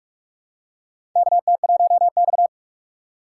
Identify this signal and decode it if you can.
Morse code, heard as a keyed tone, about 44 wpm, 690 Hz. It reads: KT1X
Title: USA Call Signs - 4 Characters - 3x Character Spacing - 45wpm